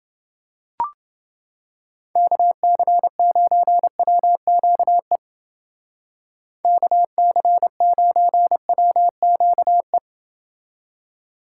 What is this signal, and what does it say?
Morse code, heard as a keyed tone, about 29 wpm, 700 Hz. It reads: KC9WQE KC9WQE